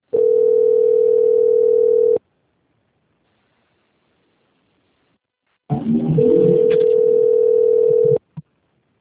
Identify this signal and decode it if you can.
Ringback tone (North America), 440 + 480 Hz.
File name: out- - - - - .28515